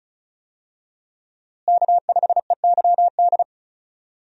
Morse code, heard as a keyed tone, about 34 wpm, 700 Hz. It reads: K5EYD
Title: USA Call Signs - 5 Characters - Repeated 2x - 35wpm